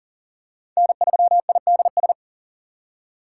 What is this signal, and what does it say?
Morse code, heard as a keyed tone, about 40 wpm, 700 Hz. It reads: N3IDS
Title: USA Call Signs - 5 Characters - 4x Character Spacing - 40wpm